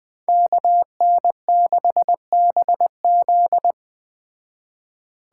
Morse code, heard as a keyed tone, about 20 wpm, 700 Hz. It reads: KN6BZ
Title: Speed Racing - USA Call Signs 17wpm